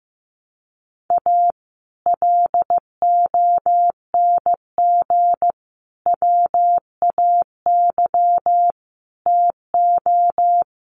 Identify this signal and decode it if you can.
Morse code, heard as a keyed tone, about 15 wpm, 700 Hz. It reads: A LONG WAY TO
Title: Sets of 5 Words 15wpm